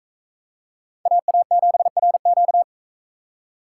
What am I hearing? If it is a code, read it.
Morse code, heard as a keyed tone, about 40 wpm, 700 Hz. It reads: AA7RQ